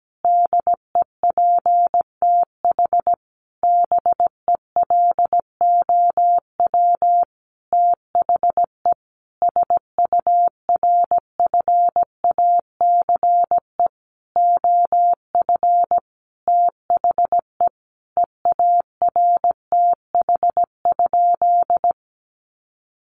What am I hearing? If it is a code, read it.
Morse code, heard as a keyed tone, about 17 wpm, 700 Hz. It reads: DEPTH BELOW THE SURFACE OF THE EARTH?
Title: Sentences from Top 1200 Words 17wpm